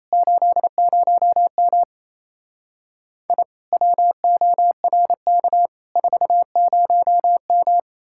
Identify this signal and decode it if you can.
Morse code, heard as a keyed tone, about 37 wpm, 700 Hz. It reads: 80M I WORK 40M